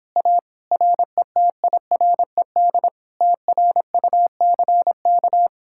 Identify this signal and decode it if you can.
Morse code, heard as a keyed tone, about 29 wpm, 700 Hz. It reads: A RETIRED TRUCK